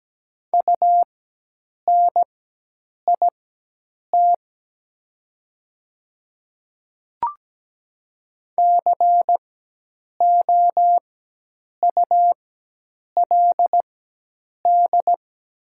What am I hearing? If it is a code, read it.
Morse code, heard as a keyed tone, about 17 wpm, spaced as for 8 wpm, 700 Hz. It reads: UNIT COULD
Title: 4x Character Spacing - Top 500 Words - 17wpm